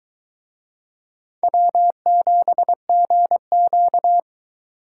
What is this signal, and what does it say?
Morse code, heard as a keyed tone, about 23 wpm, 700 Hz. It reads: W7GQ